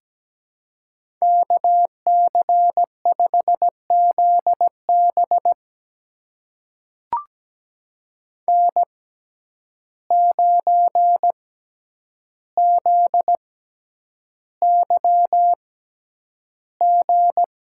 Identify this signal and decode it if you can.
Morse code, heard as a keyed tone, about 17 wpm, spaced as for 9 wpm, 700 Hz. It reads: KC5ZB N9ZYG
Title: USA Call Signs - 5 Characters - 6x Character Spacing - 17wpm